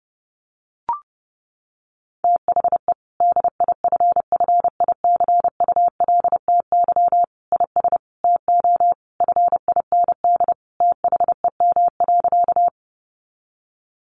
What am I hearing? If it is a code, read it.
Morse code, heard as a keyed tone, about 28 wpm, 700 Hz. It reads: THE DIFFICULTY IS TO FIND THEM.
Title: Sentences from Top 1500 Words 30wpm